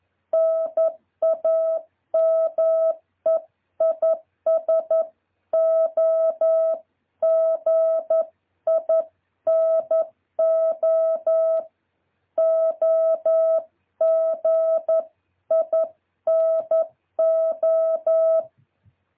Morse code, heard as a keyed tone, about 11 wpm, 650 Hz. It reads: NAME IS OGINO OGINO
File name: jh8cbh / 2023cw